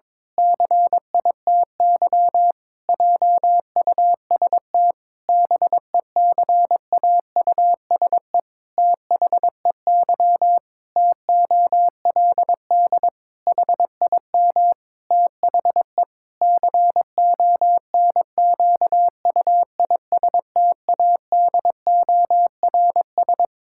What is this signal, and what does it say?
Morse code, heard as a keyed tone, about 22 wpm, 700 Hz. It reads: CITY JUST BECAUSE THEY TOLD HIM THE CONQUISTADORS
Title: Pangram Challenge v2 - 22wpm